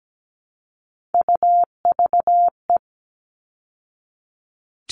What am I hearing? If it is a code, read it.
Morse code, heard as a keyed tone, about 17 wpm, 700 Hz. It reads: UVE
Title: All Cut Numbers - 3-Digit 17wpm